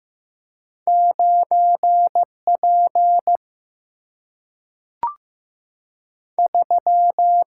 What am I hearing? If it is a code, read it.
Morse code, heard as a keyed tone, about 15 wpm, 700 Hz. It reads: 9 P 3
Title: Single-Letter Single-Number 15wpm